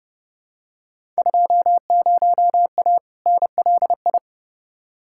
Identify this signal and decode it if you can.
Morse code, heard as a keyed tone, about 30 wpm, 700 Hz. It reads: 20A NLI
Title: Field Day Exchanges Encore - 30wpm